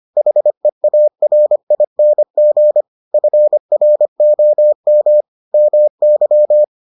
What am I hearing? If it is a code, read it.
Morse code, heard as a keyed tone, about 25 wpm, 600 Hz. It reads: HEARING FROM MY